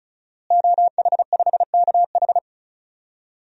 Morse code, heard as a keyed tone, about 36 wpm, 700 Hz. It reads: OH5KH
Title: DXCC-100 Call Signs - 5 Characters - Repeated 2x - 35wpm